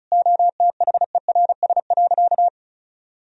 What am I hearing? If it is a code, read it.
Morse code, heard as a keyed tone, about 36 wpm, 700 Hz. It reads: OTHERS.
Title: Sentences from Top 1000 Words - Review Part 4 - 35wpm